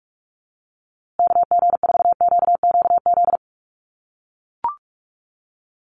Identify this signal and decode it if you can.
Morse code, heard as a keyed tone, about 44 wpm, 700 Hz. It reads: KG4QQZ
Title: USA Call Signs Repeated 2x 45wpm